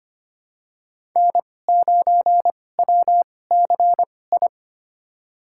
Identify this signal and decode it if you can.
Morse code, heard as a keyed tone, about 25 wpm, spaced as for 18 wpm, 700 Hz. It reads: N9WCI
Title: USA Call Signs - 5 Characters - 2x Character Spacing - 25wpm